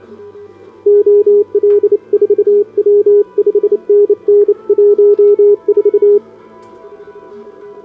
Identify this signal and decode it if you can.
Morse code, heard as a keyed tone, about 27 wpm, 410 Hz, with radio noise. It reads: OL4W 5NN 14
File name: CQWWCW2025 / SobotaOdpol 20m Hudba